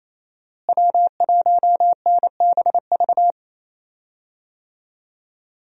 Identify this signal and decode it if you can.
Morse code, heard as a keyed tone, about 28 wpm, 700 Hz. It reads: W1NBV
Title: USA Call Signs - 5 Characters - Repeated 2x - 28wpm